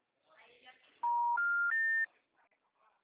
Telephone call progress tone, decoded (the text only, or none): special information tone